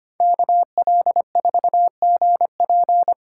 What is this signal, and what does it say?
Morse code, heard as a keyed tone, about 33 wpm, 700 Hz. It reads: KL4GP